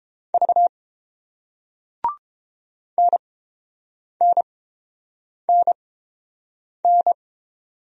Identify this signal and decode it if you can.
Morse code, heard as a keyed tone, about 30 wpm, spaced as for 7 wpm, 700 Hz. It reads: V NNNN